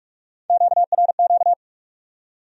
Morse code, heard as a keyed tone, about 59 wpm, spaced as for 52 wpm, 700 Hz. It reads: QRQ
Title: Speed-Racing Q-Codes - 45wpm